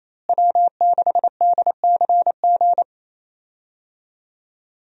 Morse code, heard as a keyed tone, about 28 wpm, 700 Hz. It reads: W6DCG
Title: USA Call Signs - 5 Characters - Repeated 2x - 28wpm